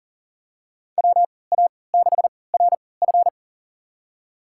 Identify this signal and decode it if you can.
Morse code, heard as a keyed tone, about 39 wpm, spaced as for 22 wpm, 700 Hz. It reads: WA6RF
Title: USA Call Signs - 5 Characters - 3x Character Spacing - 40wpm